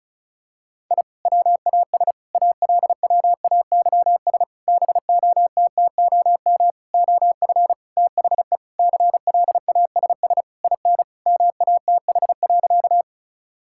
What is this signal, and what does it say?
Morse code, heard as a keyed tone, about 36 wpm, 700 Hz. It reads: I WAS ALWAYS BOTTOM OF THE CLASS IN MATH.
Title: Sentences from Top 1000 Words - Review Part 6 - 35wpm